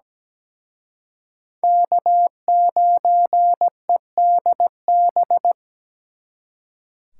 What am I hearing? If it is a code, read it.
Morse code, heard as a keyed tone, about 20 wpm, 700 Hz. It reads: K9EDB
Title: Speed Racing - USA 1x3 Call Signs 17wpm